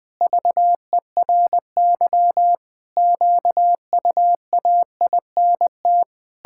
Morse code, heard as a keyed tone, about 21 wpm, 700 Hz. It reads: VERY QUAINT
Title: Pangram Challenge v2 - 20wpm